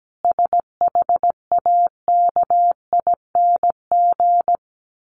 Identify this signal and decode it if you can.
Morse code, heard as a keyed tone, about 17 wpm, 700 Hz. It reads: SHAKING
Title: A Sound Of Thunder 17wpm